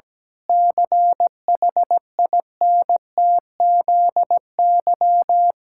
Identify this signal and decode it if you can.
Morse code, heard as a keyed tone, about 17 wpm, 700 Hz. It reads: CHINTZY
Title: Pangram Challenge v2 - 17wpm